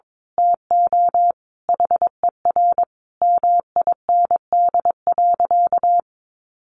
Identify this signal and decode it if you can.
Morse code, heard as a keyed tone, about 22 wpm, 700 Hz. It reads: TO HER MIND.
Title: Sentences from Top 1300 Words 22wpm